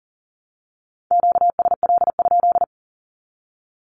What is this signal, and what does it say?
Morse code, heard as a keyed tone, about 40 wpm, 700 Hz. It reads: QSL?
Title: Q-Codes 40wpm